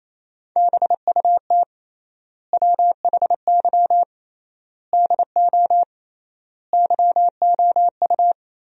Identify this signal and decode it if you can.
Morse code, heard as a keyed tone, about 27 wpm, 700 Hz. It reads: BUT WHY DO YOU
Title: 3x Word-Spacing - Sentences from Top 200 Words - 28wpm